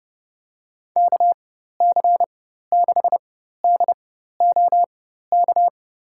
Morse code, heard as a keyed tone, about 30 wpm, spaced as for 14 wpm, 700 Hz. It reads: KC6DOK
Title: USA Call Signs - 6 Characters - 4x Character Spacing - 30wpm